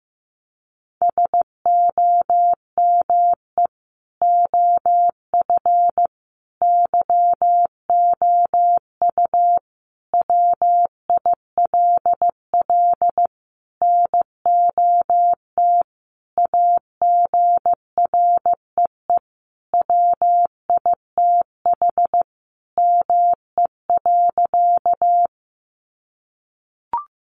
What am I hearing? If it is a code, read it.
Morse code, heard as a keyed tone, about 15 wpm, 700 Hz. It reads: SOME OF YOU WILL NOT AGREE WITH ME.
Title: Sentences from Top 1100 Words 15wpm